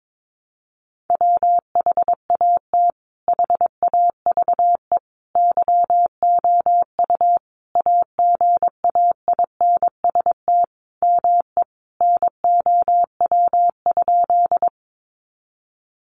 Morse code, heard as a keyed tone, about 22 wpm, 700 Hz. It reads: WHAT HAVE YOU AGAINST ME NOW?
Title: Sentences from Top 400 Words 22wpm